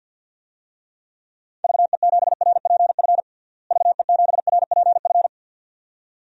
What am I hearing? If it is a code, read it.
Morse code, heard as a keyed tone, about 52 wpm, 700 Hz. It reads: VE7RPF VE7RPF
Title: DXCC-100 Call Signs - 6 Characters - Repeated 2x - 50wpm